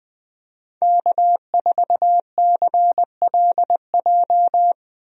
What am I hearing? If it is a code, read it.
Morse code, heard as a keyed tone, about 20 wpm, 700 Hz. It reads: K4CLJ
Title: USA Call Signs - 5 Characters - Repeated 3x - 20wpm